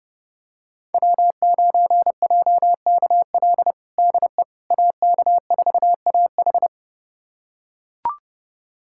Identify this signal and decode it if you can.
Morse code, heard as a keyed tone, about 30 wpm, 700 Hz. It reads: W9JKL DE AK4AH
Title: USA Callsign DE Callsign 30wpm